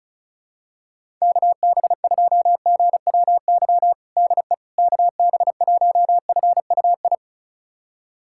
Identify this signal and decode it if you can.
Morse code, heard as a keyed tone, about 36 wpm, 700 Hz. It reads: KB2GWY DE KB1FUI
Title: USA Callsign DE Callsign - 6 Characters - 35wpm